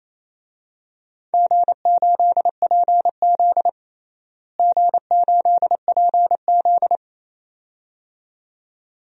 Morse code, heard as a keyed tone, about 28 wpm, 700 Hz. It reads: G8PZ G8PZ
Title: DXCC-100 Call Signs - 4 Characters - Repeated 2x - 28wpm